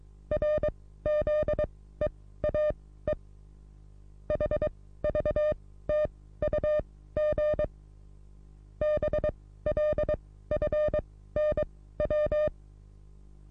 Morse code, heard as a keyed tone, about 22 wpm, spaced as for 15 wpm, 600 Hz. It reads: RZEAE HVTUG BLFNW
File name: repos / Morsekurs / DJ4UF Morsen27